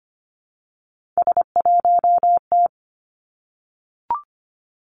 Morse code, heard as a keyed tone, about 25 wpm, 700 Hz. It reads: S1T
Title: Three Challenging Letter-Number Combinations 25wpm